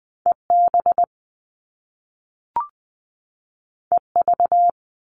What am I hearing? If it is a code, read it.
Morse code, heard as a keyed tone, about 20 wpm, 700 Hz. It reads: EB EV
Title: Top Two Letter Combinations in Words 20wpm